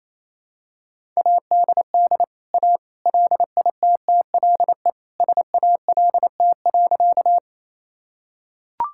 Morse code, heard as a keyed tone, about 28 wpm, 700 Hz. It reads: ADD A LITTLE SALT.
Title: Sentences from Top 1000 Words - Review Part 5 - 28wpm